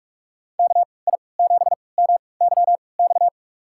Morse code, heard as a keyed tone, about 46 wpm, spaced as for 25 wpm, 700 Hz. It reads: KI7MYX
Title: USA Call Signs - 6 Characters - 3x Character Spacing - 45wpm